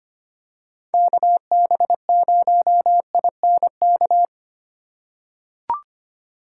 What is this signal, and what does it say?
Morse code, heard as a keyed tone, about 25 wpm, 700 Hz. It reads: KB0INK
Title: USA Call Signs - 4x Character Spacing - 25wpm